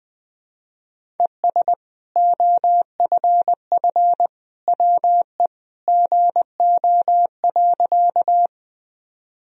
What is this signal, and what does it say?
Morse code, heard as a keyed tone, about 20 wpm, 700 Hz. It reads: ES OFF WE GO.
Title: Sentences from Top 200 Words and Abbreviations 20wpm